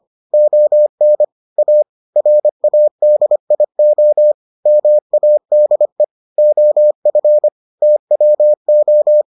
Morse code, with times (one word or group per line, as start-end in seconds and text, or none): 0.3-1.2 ON
1.6-1.8 A
2.2-4.3 RADIO
4.7-6.0 MADE
6.4-7.5 OF
7.8-9.2 TWO